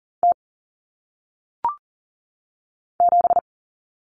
Morse code, heard as a keyed tone, about 40 wpm, 700 Hz. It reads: T 7
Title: Common Cut Numbers - 1-Digit 40wpm